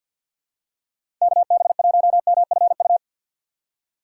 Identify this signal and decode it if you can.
Morse code, heard as a keyed tone, about 52 wpm, 700 Hz. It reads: KB1MFU
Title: USA Call Signs - 6 Characters - 3x Character Spacing - 50wpm